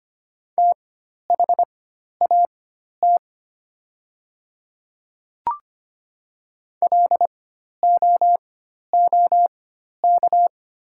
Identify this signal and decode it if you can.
Morse code, heard as a keyed tone, about 25 wpm, spaced as for 12 wpm, 700 Hz. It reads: THAT LOOK
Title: 4x Character Spacing - Top 100 Words - 25wpm